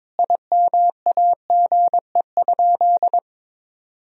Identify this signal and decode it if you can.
Morse code, heard as a keyed tone, about 22 wpm, 700 Hz. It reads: IMAGE?
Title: Sentences from Top 1000 Words - Review Part 4 - 22wpm